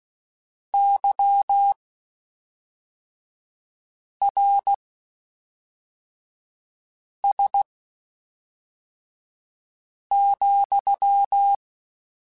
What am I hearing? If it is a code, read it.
Morse code, heard as a keyed tone, about 16 wpm, spaced as for 3 wpm, 800 Hz. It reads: YRS,
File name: QSOs / QSOs-mp3s / WD0BGP 1@5